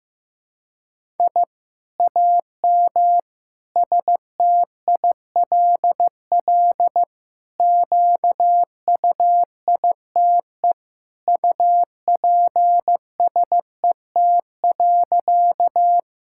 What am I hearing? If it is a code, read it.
Morse code, heard as a keyed tone, about 16 wpm, 700 Hz. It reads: I AM STILL QUITE UPSET.
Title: Sentences from Top 1000 Words - Review Part 5 - 15wpm